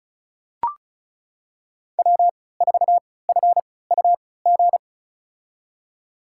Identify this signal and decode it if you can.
Morse code, heard as a keyed tone, about 36 wpm, spaced as for 20 wpm, 700 Hz. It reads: W4FUG
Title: USA Call Signs - 5 Characters - 3x Character Spacing - 35wpm